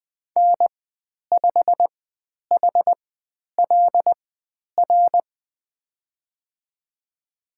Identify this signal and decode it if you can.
Morse code, heard as a keyed tone, about 20 wpm, spaced as for 10 wpm, 700 Hz. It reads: N5HLR
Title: 095 Lesson 16g - Letter c - Call Signs 20wpm